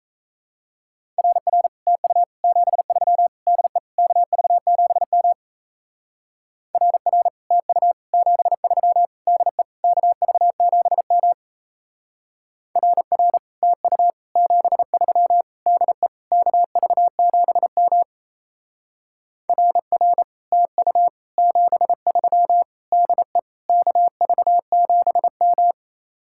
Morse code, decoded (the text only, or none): RR TU 73 DE KV7M RR TU 73 DE KV7M RR TU 73 DE KV7M RR TU 73 DE KV7M